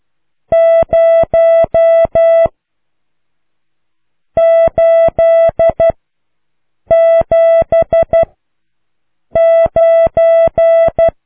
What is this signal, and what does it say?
Morse code, heard as a keyed tone, about 12 wpm, 660 Hz. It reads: 0 8 7 9